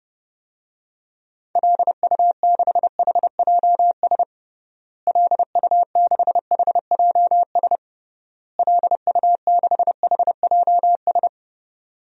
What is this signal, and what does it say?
Morse code, heard as a keyed tone, about 30 wpm, 700 Hz. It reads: LU6HJS LU6HJS LU6HJS
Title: DXCC-100 Call Signs - 6 Characters - Repeated 3x - 30wpm